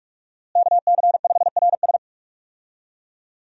Morse code, heard as a keyed tone, about 46 wpm, 700 Hz. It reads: KC5RS
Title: USA Call Signs - 5 Characters - Repeated 2x - 45wpm